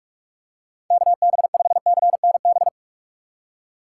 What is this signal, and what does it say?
Morse code, heard as a keyed tone, about 45 wpm, 700 Hz. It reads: KB5CNB